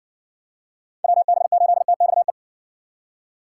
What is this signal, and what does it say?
Morse code, heard as a keyed tone, about 58 wpm, 700 Hz. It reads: WB8TXE